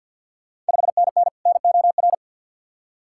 Morse code, heard as a keyed tone, about 53 wpm, 690 Hz. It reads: 5NN NOR
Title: Ontario QSO Party Exchange 50wpm